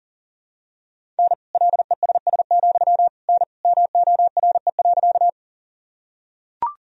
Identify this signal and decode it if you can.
Morse code, heard as a keyed tone, about 40 wpm, 700 Hz. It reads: N LESS, N MORE.
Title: Sentences from Top 400 Words and Abbreviations 40wpm